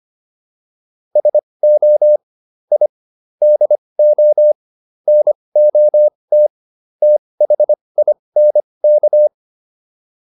Morse code, heard as a keyed tone, about 25 wpm, spaced as for 20 wpm, 600 Hz. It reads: SO I DO NOT THINK